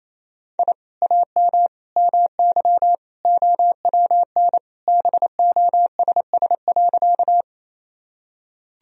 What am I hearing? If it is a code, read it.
Morse code, heard as a keyed tone, about 28 wpm, 700 Hz. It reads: I AM MY OWN BOSS.
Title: Sentences from Top 1000 Words - Review Part 4 - 28wpm